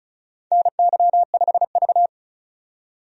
Morse code, heard as a keyed tone, about 34 wpm, 700 Hz. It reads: NY5V
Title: USA Call Signs - 4 Characters - 35wpm